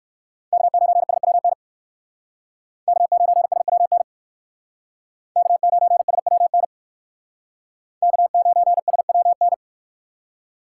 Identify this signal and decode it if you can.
Morse code, heard as a keyed tone, about 60 wpm, 700 Hz. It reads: K9SWN K9SWN K9SWN K9SWN